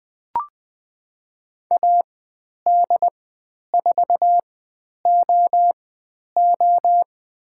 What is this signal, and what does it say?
Morse code, heard as a keyed tone, about 20 wpm, spaced as for 10 wpm, 700 Hz. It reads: AD4OO
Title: 109 Lesson 18g - Letter m - Call Signs 20wpm